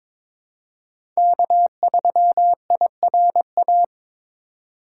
Morse code, heard as a keyed tone, about 22 wpm, 700 Hz. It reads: K3IRA
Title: USA Call Signs - 5 Characters - 3x Character Spacing - 22wpm